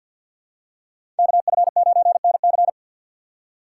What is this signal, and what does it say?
Morse code, heard as a keyed tone, about 51 wpm, 700 Hz. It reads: KF9NC
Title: USA Call Signs - 5 Characters - 6x Character Spacing - 50wpm